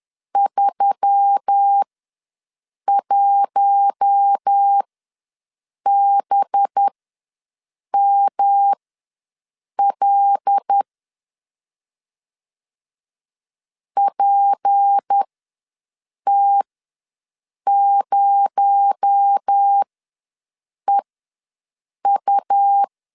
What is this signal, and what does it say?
Morse code, heard as a keyed tone, about 10 wpm, spaced as for 6 wpm, 800 Hz. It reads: 31BML PT0EU
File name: LEZIONI / 10 SIGNS / 10D COMA ws 10wpm 4314